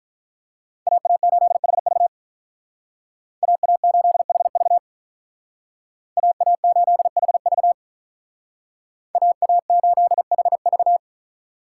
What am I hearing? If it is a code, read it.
Morse code, heard as a keyed tone, about 46 wpm, 700 Hz. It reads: AA8HV AA8HV AA8HV AA8HV